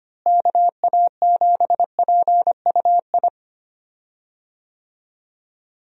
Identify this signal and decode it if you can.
Morse code, heard as a keyed tone, about 25 wpm, 700 Hz. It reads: KA7PUI
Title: USA Call Signs - 6 Characters - Repeated 3x - 25wpm